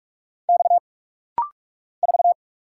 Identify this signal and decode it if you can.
Morse code, heard as a keyed tone, about 48 wpm, 700 Hz. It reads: X 4
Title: Single Letter-Number - Rapid-Fire 45wpm